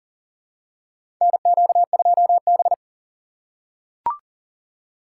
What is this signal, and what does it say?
Morse code, heard as a keyed tone, about 40 wpm, 700 Hz. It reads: NQ2B
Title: USA Call Signs - 4 Characters - 3x Character Spacing - 40wpm